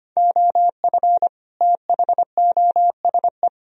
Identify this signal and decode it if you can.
Morse code, heard as a keyed tone, about 33 wpm, 700 Hz. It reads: OF THOSE